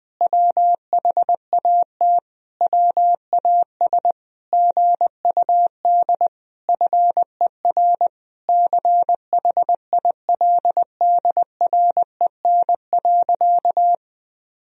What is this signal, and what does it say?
Morse code, heard as a keyed tone, about 20 wpm, 700 Hz. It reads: WHAT WAS GUD FER CHILDREN.
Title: Sentences from Top 400 Words and Abbreviations 20wpm